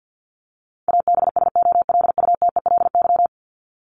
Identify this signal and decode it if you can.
Morse code, heard as a keyed tone, about 50 wpm, 700 Hz. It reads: ABSOLUTELY
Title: Top 1501-1750 Words 50wpm